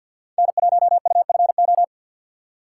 Morse code, heard as a keyed tone, about 53 wpm, 700 Hz. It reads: N1UFO
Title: USA Call Signs - 5 Characters - 3x Character Spacing - 50wpm